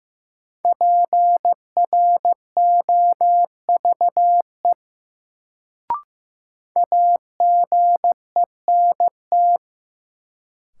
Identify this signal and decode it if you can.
Morse code, heard as a keyed tone, about 16 wpm, 700 Hz. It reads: PROVE AGENT
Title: Sentences from Top 1600 Words - New Word Drill Alt - 15wpm